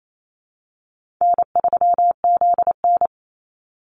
Morse code, heard as a keyed tone, about 28 wpm, 700 Hz. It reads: N3ZN
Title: Easy Call Signs Repeated 3x - 28wpm